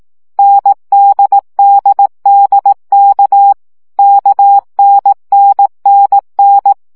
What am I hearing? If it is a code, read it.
Morse code, heard as a keyed tone, about 18 wpm, 800 Hz. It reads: NDDDK KNNNN
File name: CW gyakorlas n d k 15 baud